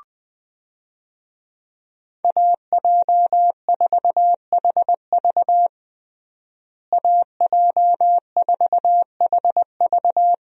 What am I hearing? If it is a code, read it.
Morse code, heard as a keyed tone, about 21 wpm, 700 Hz. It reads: AJ4HV AJ4HV
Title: USA Call Signs - 5 Characters - Repeated 3x - 20wpm